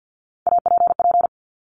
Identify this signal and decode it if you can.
Morse code, heard as a keyed tone, about 50 wpm, 700 Hz. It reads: APP
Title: Top Three Letter Combinations in Words 50wpm